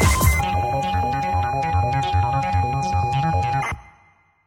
ZX Spectrum tape loading.